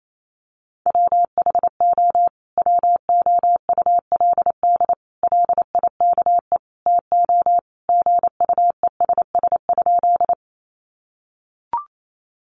Morse code, heard as a keyed tone, about 28 wpm, 700 Hz. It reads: WHO WOULD LIKE TO GUESS?
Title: Sentences from Top 1500 Words 28wpm